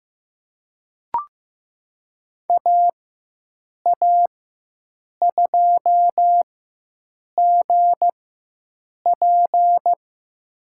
Morse code, heard as a keyed tone, about 15 wpm, spaced as for 7 wpm, 700 Hz. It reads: AA2GP
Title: USA Call Signs - 4x Character Spacing - 15wpm